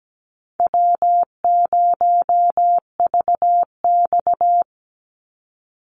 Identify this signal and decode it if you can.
Morse code, heard as a keyed tone, about 17 wpm, 700 Hz. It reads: W0VX
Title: Call Signs Repeated 2x - 17wpm